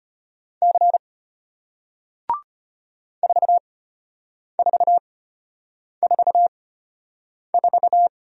Morse code, heard as a keyed tone, about 34 wpm, spaced as for 8 wpm, 700 Hz. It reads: C 4444